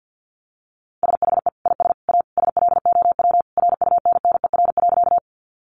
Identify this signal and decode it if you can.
Morse code, heard as a keyed tone, about 50 wpm, 700 Hz. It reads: SHE IS A SLOW RUNNER.
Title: Additional Easy Phrases 50wpm